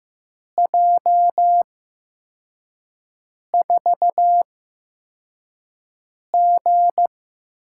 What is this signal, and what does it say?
Morse code, heard as a keyed tone, about 15 wpm, spaced as for 4 wpm, 700 Hz. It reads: J4G
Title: USA Call Signs - 4 Characters - 8x Character Spacing - 15wpm